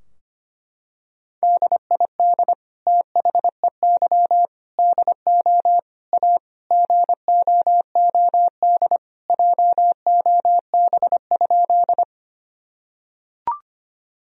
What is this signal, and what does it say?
Morse code, heard as a keyed tone, about 25 wpm, 700 Hz. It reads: DID THEY DO A GOOD JOB?
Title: Sentences from Top 1000 Words - Review Part 3 - 25wpm